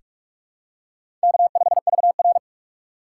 Morse code, heard as a keyed tone, about 44 wpm, 700 Hz. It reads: K5VR
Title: K1USN SST QSO Practice - 45wpm